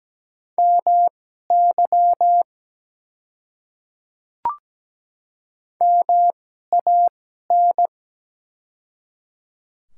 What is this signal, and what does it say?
Morse code, heard as a keyed tone, about 17 wpm, spaced as for 12 wpm, 700 Hz. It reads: MY MAN
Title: 2x Character Spacing - Top 200 Words - 17wpm